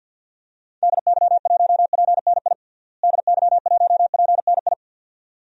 Morse code, heard as a keyed tone, about 53 wpm, 700 Hz. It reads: DY1PNI DY1PNI
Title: DXCC-100 Call Signs - 6 Characters - Repeated 2x - 50wpm